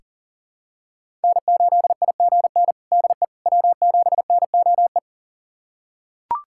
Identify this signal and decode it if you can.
Morse code, heard as a keyed tone, about 40 wpm, 700 Hz. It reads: N8IGN DE W7NOE
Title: USA Callsign DE Callsign - 5 Characters - 40wpm